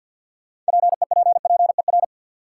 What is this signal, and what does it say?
Morse code, heard as a keyed tone, about 53 wpm, 700 Hz. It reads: PEPPER
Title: Sets of 5 Words - New Word Drill - 50wpm